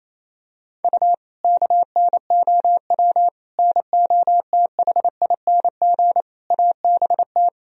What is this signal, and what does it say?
Morse code, heard as a keyed tone, about 28 wpm, 700 Hz. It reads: U KNOW NOTHING ABT
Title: Sentences from Top 500 Words and Abbreviations 28wpm